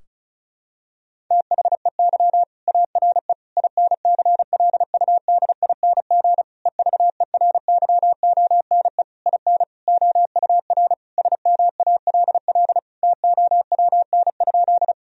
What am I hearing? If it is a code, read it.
Morse code, heard as a keyed tone, about 36 wpm, 700 Hz. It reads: THEY ARE INCLUDING EVERYONE IN OUR SMALL TOWN?
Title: Sentences from Top 1000 Words - Review Part 4 - 35wpm